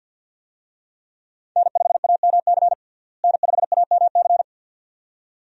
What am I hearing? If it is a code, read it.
Morse code, heard as a keyed tone, about 52 wpm, 700 Hz. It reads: N5AMC N5AMC
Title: USA Call Signs - 5 Characters - Repeated 2x - 50wpm